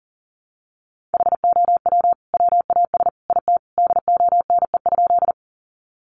Morse code, heard as a keyed tone, about 40 wpm, 700 Hz. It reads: HOW WAS IT DONE?
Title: Sentences from Top 500 Words 40wpm